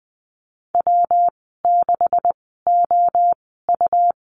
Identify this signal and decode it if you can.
Morse code, heard as a keyed tone, about 20 wpm, spaced as for 14 wpm, 700 Hz. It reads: W6OU
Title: 213 Lesson 34g - Number 8 - Call Signs 20wpm